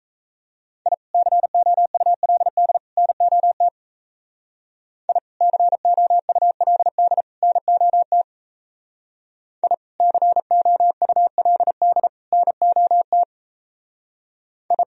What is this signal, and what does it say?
Morse code, heard as a keyed tone, about 36 wpm, 700 Hz. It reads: I COULD NOT I COULD NOT I COULD NOT I